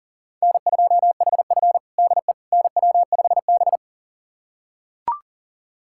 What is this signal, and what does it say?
Morse code, heard as a keyed tone, about 40 wpm, 700 Hz. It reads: N2HF DE NW5B
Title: USA Callsign DE Callsign - 4 Characters - 40wpm